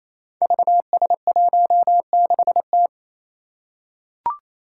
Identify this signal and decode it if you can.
Morse code, heard as a keyed tone, about 28 wpm, 700 Hz. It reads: VS16T
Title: Five Challenging Letter-Number Combinations 28wpm